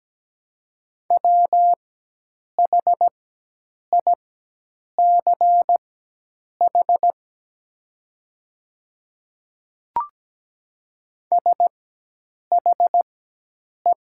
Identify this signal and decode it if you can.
Morse code, heard as a keyed tone, about 17 wpm, spaced as for 8 wpm, 700 Hz. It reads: WHICH SHE
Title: 4x Character Spacing - Top 200 Words - 17wpm